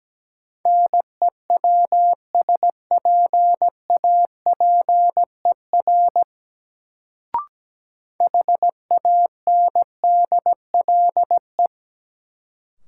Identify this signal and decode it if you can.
Morse code, heard as a keyed tone, about 17 wpm, 700 Hz. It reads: NEWSPAPER HANDLE
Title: Sentences from Top 1100 Words - New Word Drill Alt - 17wpm